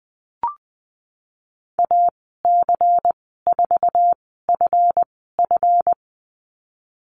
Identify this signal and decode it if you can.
Morse code, heard as a keyed tone, about 20 wpm, spaced as for 15 wpm, 700 Hz. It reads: AC4FF
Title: 167 Lesson 27g - Number 7 - Call Signs 20wpm